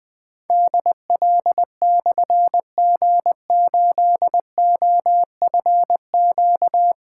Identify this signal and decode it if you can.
Morse code, heard as a keyed tone, about 20 wpm, 700 Hz. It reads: DL/G8OFQ
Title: Difficult Call Signs 20wpm